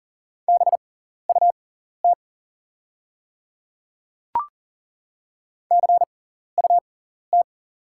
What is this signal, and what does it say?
Morse code, heard as a keyed tone, about 40 wpm, spaced as for 14 wpm, 690 Hz. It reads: BUT CUT
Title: 6x Character Spacing - Top 500 Words - 40wpm